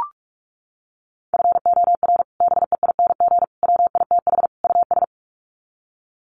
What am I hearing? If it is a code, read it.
Morse code, heard as a keyed tone, about 44 wpm, 700 Hz. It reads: FOR BEING WITH US
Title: Sets of 4 Words 45wpm